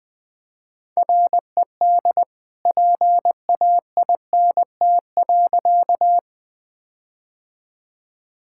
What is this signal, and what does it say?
Morse code, heard as a keyed tone, about 20 wpm, 700 Hz. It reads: RED PAINT.
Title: All-Age Appropriate Jokes - 20wpm